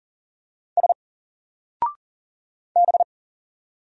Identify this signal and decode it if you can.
Morse code, heard as a keyed tone, about 40 wpm, 700 Hz. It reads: S B
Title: Single Character (Letter:Number:Punctuation:Prosign) 40wpm